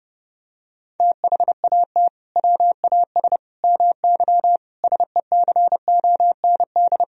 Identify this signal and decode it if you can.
Morse code, heard as a keyed tone, about 32 wpm, 700 Hz. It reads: THAT WAS MY SECOND
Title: Sentences from Top 1000 Words - Review Part 4 - 30wpm